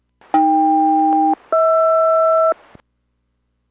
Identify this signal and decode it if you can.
Aircraft SELCAL, heard as two pairs of tones. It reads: AK-HR